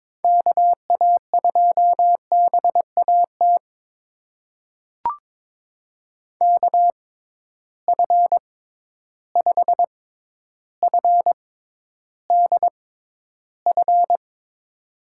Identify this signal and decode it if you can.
Morse code, heard as a keyed tone, about 22 wpm, spaced as for 11 wpm, 700 Hz. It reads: KA2BAT KF5FDF
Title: USA Call Signs - 6 Characters - 6x Character Spacing - 22wpm